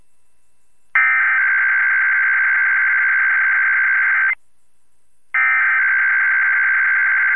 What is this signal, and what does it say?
Emergency Alert System (SAME) tones (header bursts), heard as harsh buzzing bursts. It is heard from 1.0 to 7.0 s.